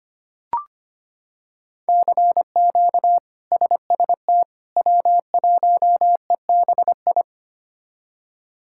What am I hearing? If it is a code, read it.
Morse code, heard as a keyed tone, about 25 wpm, 700 Hz. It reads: CQ SST W1EBI
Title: K1USN SST QSO Practice - 25wpm